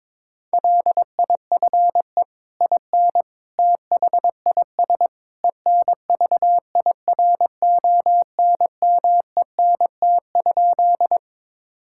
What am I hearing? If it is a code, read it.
Morse code, heard as a keyed tone, about 22 wpm, 700 Hz. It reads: LIFE IN THIS ENVIRONMENT?
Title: Sentences from Top 1000 Words - Review Part 5 - 22wpm